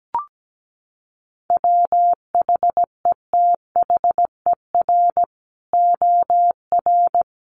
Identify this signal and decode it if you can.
Morse code, heard as a keyed tone, about 17 wpm, 700 Hz. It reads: WHETHER OR
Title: Sets of 3 Words 17wpm